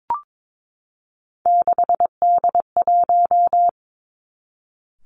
Morse code, heard as a keyed tone, about 22 wpm, 700 Hz. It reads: 6D1
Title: Three Challenging Letter-Number Combinations 22wpm